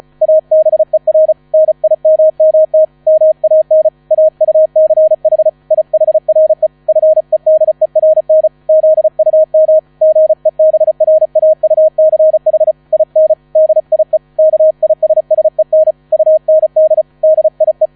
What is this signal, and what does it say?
Morse code, heard as a keyed tone, about 34 wpm, 630 Hz. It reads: ABER NIMMT MAN AUCH IHRE FEDERN ZUM GEBRAUCH IN DIE KISSEN UND DIE
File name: 35wpm